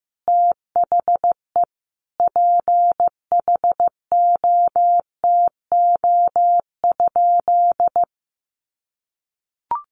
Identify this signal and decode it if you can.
Morse code, heard as a keyed tone, about 15 wpm, 700 Hz. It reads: THE PHOTO?
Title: Sentences from Top 1200 Words 15wpm